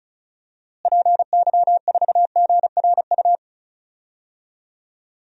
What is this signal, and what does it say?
Morse code, heard as a keyed tone, about 36 wpm, 700 Hz. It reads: PY4GRU
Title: DXCC-100 Call Signs - 6 Characters - Repeated 2x - 35wpm